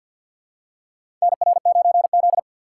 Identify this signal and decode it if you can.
Morse code, heard as a keyed tone, about 53 wpm, 700 Hz. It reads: NR9Z
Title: USA Call Signs - 4 Characters - 6x Character Spacing - 50wpm